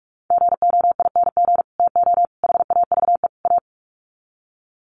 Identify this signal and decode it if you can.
Morse code, heard as a keyed tone, about 45 wpm, 700 Hz. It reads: GOING TO HAVE A
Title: Sets of 4 Words 45wpm